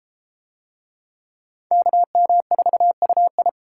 Morse code, heard as a keyed tone, about 30 wpm, 700 Hz. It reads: KM4UI